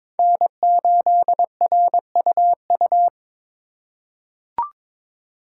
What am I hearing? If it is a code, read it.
Morse code, heard as a keyed tone, about 22 wpm, 700 Hz. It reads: N8RUU
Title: USA Call Signs - 5 Characters - 2x Character Spacing - 22wpm